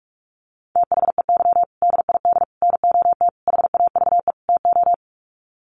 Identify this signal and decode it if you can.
Morse code, heard as a keyed tone, about 44 wpm, 700 Hz. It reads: THEY DID NOT HAVE TO
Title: Sets of 5 Words Encore2 45wpm